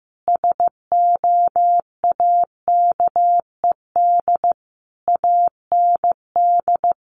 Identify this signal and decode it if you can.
Morse code, heard as a keyed tone, about 15 wpm, 700 Hz. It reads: SOAKED AND
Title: A Sound Of Thunder 15wpm